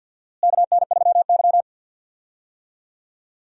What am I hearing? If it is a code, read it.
Morse code, heard as a keyed tone, about 51 wpm, 700 Hz. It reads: KN3Y
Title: USA Call Signs - 4 Characters - Repeated 3x - 50wpm